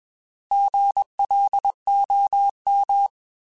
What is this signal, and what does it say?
Morse code, heard as a keyed tone, about 21 wpm, 800 Hz. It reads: GLOM
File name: Alphabetical / Alphabetical mp3s / Gs @20